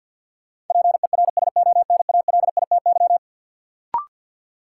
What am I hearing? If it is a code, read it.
Morse code, heard as a keyed tone, about 50 wpm, 700 Hz. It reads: PERSONALITY
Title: Sentences from Top 1400 Words - New Word Drill - 50wpm